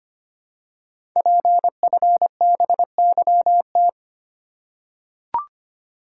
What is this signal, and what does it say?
Morse code, heard as a keyed tone, about 25 wpm, 700 Hz. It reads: PFBYT
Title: Code Groups - 5 Letters - 25wpm